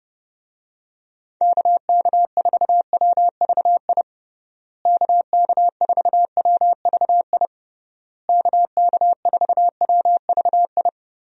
Morse code, heard as a keyed tone, about 30 wpm, 700 Hz. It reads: KK4WVI KK4WVI KK4WVI
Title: USA Call Signs - 6 Characters - Repeated 3x - 30wpm